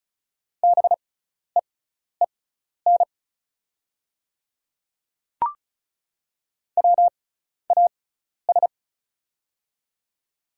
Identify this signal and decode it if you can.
Morse code, heard as a keyed tone, about 36 wpm, spaced as for 12 wpm, 690 Hz. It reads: BEEN WAS
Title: 6x Character Spacing - Top 200 Words - 35wpm